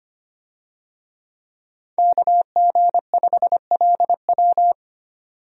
Morse code, heard as a keyed tone, about 25 wpm, 690 Hz. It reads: KG5LW
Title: USA Call Signs - 5 Characters - 25wpm